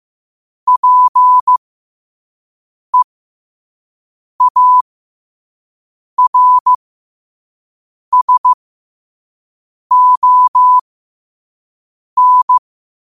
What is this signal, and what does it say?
Morse code, heard as a keyed tone, about 15 wpm, spaced as for 5 wpm, 1000 Hz. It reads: PEARSON
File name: qotdfiles / 5wpm